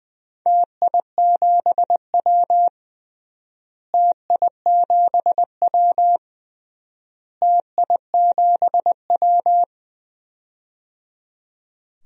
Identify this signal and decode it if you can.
Morse code, heard as a keyed tone, about 20 wpm, 700 Hz. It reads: TI7W TI7W TI7W
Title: DXCC-100 Call Signs - 4 Characters - Repeated 3x - 20wpm